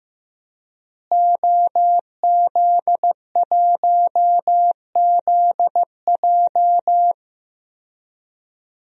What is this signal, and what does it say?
Morse code, heard as a keyed tone, about 15 wpm, 700 Hz. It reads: OZ1ZJ
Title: DXCC-100 Call Signs - 5 Characters - Repeated 3x - 15wpm